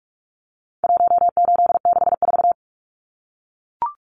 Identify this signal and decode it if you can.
Morse code, heard as a keyed tone, about 44 wpm, 700 Hz. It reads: 1864
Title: Numbers - 4-Digit 45wpm